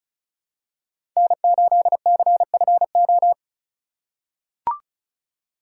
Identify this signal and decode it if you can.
Morse code, heard as a keyed tone, about 34 wpm, 700 Hz. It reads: N8CFO
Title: USA Call Signs - 5 Characters - 3x Character Spacing - 35wpm